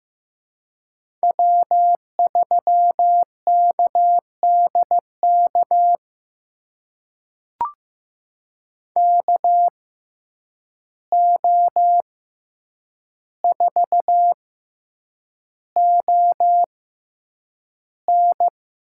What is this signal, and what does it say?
Morse code, heard as a keyed tone, about 15 wpm, spaced as for 8 wpm, 700 Hz. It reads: W3KDK KO4ON
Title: USA Call Signs - 5 Characters - 6x Character Spacing - 15wpm